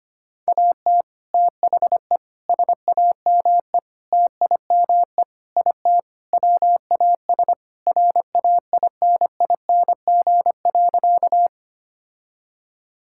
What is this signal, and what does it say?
Morse code, heard as a keyed tone, about 25 wpm, 700 Hz. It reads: AT THE SAME TIME IT WAS RAINING.
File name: practice / 25wpm / Sentences with 3-Word Phrases - 25wpm